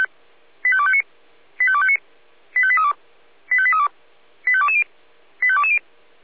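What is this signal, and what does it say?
Five-tone selective call (ZVEI), heard as quick runs of five tones, tones of about 70 ms. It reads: 85259 85259 85832 85832 85220 85220
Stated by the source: ZVEI five-tone selective call received with an SDR.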